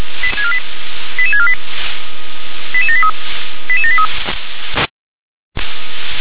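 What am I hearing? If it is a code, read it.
Five-tone selective call (ZVEI), heard as quick runs of five tones, tones of about 70 ms. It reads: 88648 88783 88783
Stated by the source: ZVEI five-tone selective call received on SDR.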